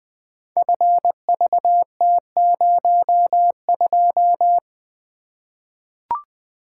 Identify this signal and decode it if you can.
Morse code, heard as a keyed tone, about 20 wpm, 700 Hz. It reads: FVT02
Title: Code Groups - 5 Letter:Numbers - 20wpm